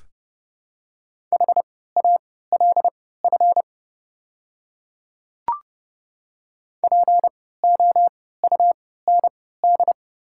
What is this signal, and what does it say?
Morse code, heard as a keyed tone, about 30 wpm, spaced as for 17 wpm, 700 Hz. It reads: HALF POUND
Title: 3x Character Spacing - Top 500 Words - 30wpm